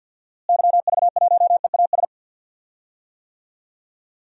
Morse code, heard as a keyed tone, about 46 wpm, 700 Hz. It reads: YV1EAS
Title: DXCC-100 Call Signs - 6 Characters - Repeated 2x - 50wpm